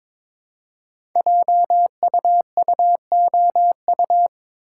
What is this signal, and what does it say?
Morse code, heard as a keyed tone, about 23 wpm, 700 Hz. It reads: JUUOU
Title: Code Groups - 5 Letters - 22wpm